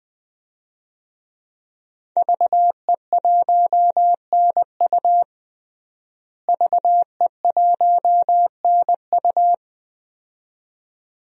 Morse code, heard as a keyed tone, about 20 wpm, 700 Hz. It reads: VE1NU VE1NU
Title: DXCC-100 Call Signs - 5 Characters - Repeated 2x - 20wpm